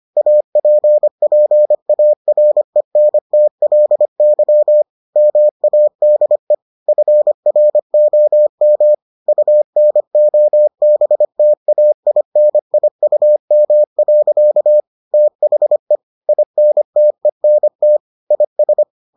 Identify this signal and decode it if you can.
Morse code, heard as a keyed tone, about 25 wpm, 600 Hz. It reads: APPARENTLY MADE FROM UNOBTAINIUM. THE INTENT IS